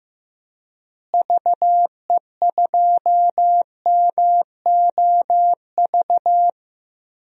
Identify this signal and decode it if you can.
Morse code, heard as a keyed tone, about 15 wpm, 700 Hz. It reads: VE2MOV
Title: DXCC-100 Call Signs - 6 Characters - Repeated 3x - 15wpm